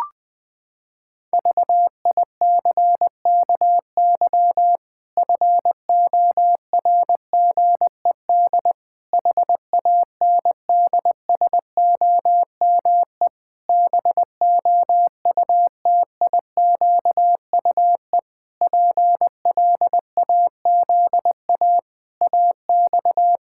Morse code, heard as a keyed tone, about 19 wpm, 700 Hz. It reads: VICKY FORGED HANDSOME BOUTIQUE PLAZA AX
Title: Pangram Challenge v2 - 20wpm